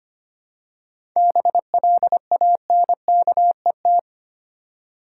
Morse code, heard as a keyed tone, about 25 wpm, 700 Hz. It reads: BLANKET
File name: practice / 25wpm / Easy Words with Hints - 25wpm